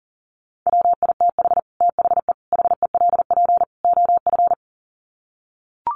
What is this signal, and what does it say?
Morse code, heard as a keyed tone, about 40 wpm, 700 Hz. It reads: WITH THE HELP OF
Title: Sets of 4 Words 40wpm